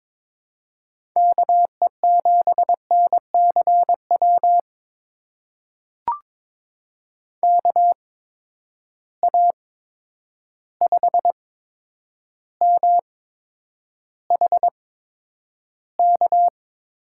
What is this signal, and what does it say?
Morse code, heard as a keyed tone, about 22 wpm, spaced as for 9 wpm, 700 Hz. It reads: KE7NCW KA5MHK
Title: USA Call Signs - 6 Characters - 8x Character Spacing - 22wpm